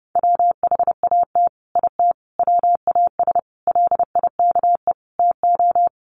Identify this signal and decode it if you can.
Morse code, heard as a keyed tone, about 30 wpm, 700 Hz. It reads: WHAT IT WAS LIKE TO
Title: Sets of 5 Words Encore 30wpm